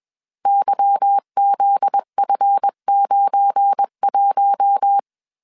Morse code, heard as a keyed tone, about 21 wpm, 800 Hz. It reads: YZF91